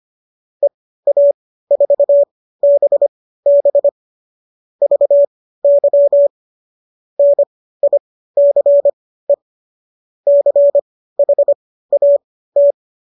Morse code, heard as a keyed tone, about 25 wpm, spaced as for 15 wpm, 570 Hz. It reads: EA4BB VY NICE CHAT